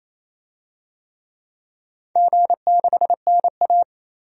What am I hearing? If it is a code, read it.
Morse code, heard as a keyed tone, about 28 wpm, 700 Hz. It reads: G6NA
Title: DXCC-100 Call Signs - 4 Characters - Repeated 2x - 28wpm